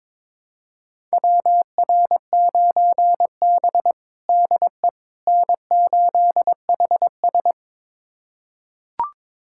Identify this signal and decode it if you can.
Morse code, heard as a keyed tone, about 22 wpm, 700 Hz. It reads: WR9B DE N8HS
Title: USA Callsign DE Callsign - 4 Characters - 22wpm